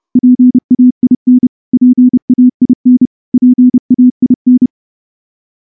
Morse code, heard as a keyed tone, about 29 wpm, 250 Hz. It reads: PAIN PAIN PAIN
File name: morse-training-30WPM